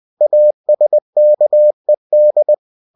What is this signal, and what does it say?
Morse code, heard as a keyed tone, about 20 wpm, 600 Hz. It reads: ASKED
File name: cw-books / winnie-the-pooh / winnie-20wpm-page-0011